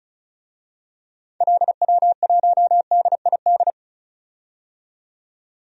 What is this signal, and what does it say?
Morse code, heard as a keyed tone, about 36 wpm, 700 Hz. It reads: LW1DID
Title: DXCC-100 Call Signs - 6 Characters - 35wpm